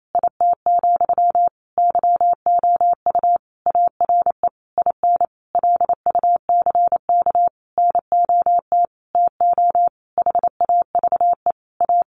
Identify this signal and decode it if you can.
Morse code, heard as a keyed tone, about 28 wpm, 700 Hz. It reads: IT, YOU ARE IN LUCK NOT TO HAVE A
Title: Sentences from Top 1500 Words 28wpm